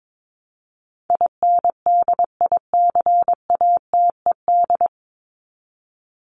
Morse code, heard as a keyed tone, about 22 wpm, 700 Hz. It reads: INDICATED
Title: ED Suffix - Top 100 Words with Base Words 22wpm